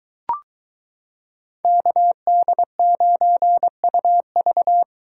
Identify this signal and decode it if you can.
Morse code, heard as a keyed tone, about 23 wpm, 700 Hz. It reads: KD9UV